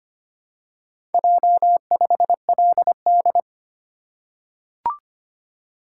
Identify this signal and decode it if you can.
Morse code, heard as a keyed tone, about 25 wpm, 700 Hz. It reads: J5LD
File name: practice / 25wpm / Four Letter-Number Combinations - 25wpm